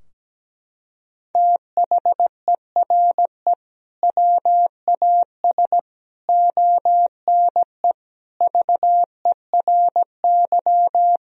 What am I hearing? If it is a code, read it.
Morse code, heard as a keyed tone, about 17 wpm, 700 Hz. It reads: THERE WAS ONE VERY
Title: Sentences from Top 1000 Words - Review Part 3 - 17wpm